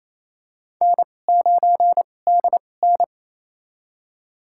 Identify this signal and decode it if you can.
Morse code, heard as a keyed tone, about 28 wpm, spaced as for 20 wpm, 700 Hz. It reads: N9DN
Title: USA Call Signs - 2x Character Spacing - 28wpm